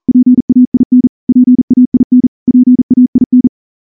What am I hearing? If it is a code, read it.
Morse code, heard as a keyed tone, about 38 wpm, 250 Hz. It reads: PAIN PAIN PAIN
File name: morse / morse-training-40WPM